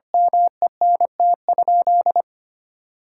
Morse code, heard as a keyed tone, about 25 wpm, 700 Hz. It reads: MENT?
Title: Sentences from Top 1600 Words 25wpm